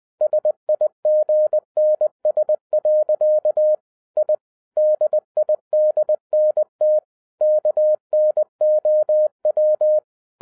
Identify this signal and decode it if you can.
Morse code, heard as a keyed tone, about 20 wpm, 610 Hz. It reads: SIGNS. I DIDNT KNOW